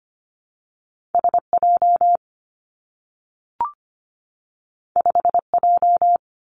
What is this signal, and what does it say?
Morse code, heard as a keyed tone, about 25 wpm, 700 Hz. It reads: S J 5 J
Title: Two Challenging Letter-Number Combinations 25wpm